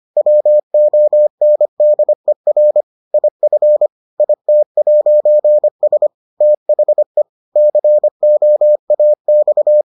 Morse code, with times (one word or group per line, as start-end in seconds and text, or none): 0.2-2.8 WONDER
3.1-3.9 IF
4.2-6.1 IT'S
6.4-7.2 THE
7.6-9.8 COAX